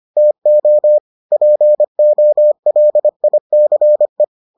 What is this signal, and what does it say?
Morse code, heard as a keyed tone, about 25 wpm, 600 Hz. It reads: TO POLICE